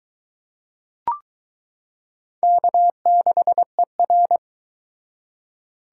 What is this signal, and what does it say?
Morse code, heard as a keyed tone, about 22 wpm, 700 Hz. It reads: K6ER